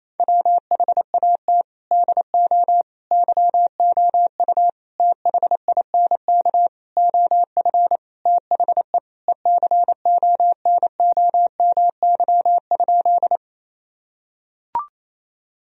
Morse code, heard as a keyed tone, about 27 wpm, 700 Hz. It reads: WHAT DO YOU THINK OF THE ECONOMY?
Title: Sentences from Top 1000 Words - Review Part 3 - 28wpm